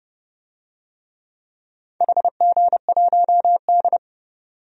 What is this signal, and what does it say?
Morse code, heard as a keyed tone, about 30 wpm, 700 Hz. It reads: HG1D
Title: DXCC-100 Call Signs - 4 Characters - 30wpm